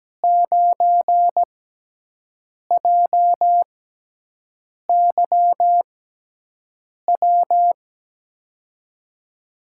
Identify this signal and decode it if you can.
Morse code, heard as a keyed tone, about 17 wpm, spaced as for 6 wpm, 700 Hz. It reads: 9JYW
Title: USA Call Signs - 6 Characters - 6x Character Spacing - 17wpm